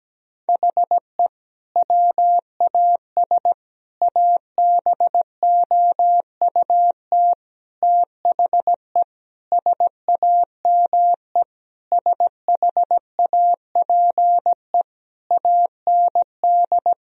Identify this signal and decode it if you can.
Morse code, heard as a keyed tone, about 17 wpm, 700 Hz. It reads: HE WAS ABOUT THE SAME SHAPE AND
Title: Sentences from Top 1000 Words - Review Part 3 - 17wpm